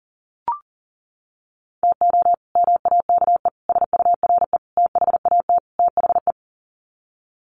Morse code, heard as a keyed tone, about 40 wpm, 700 Hz. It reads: TO MAKE SURE THAT THE
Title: Sets of 5 Words 40wpm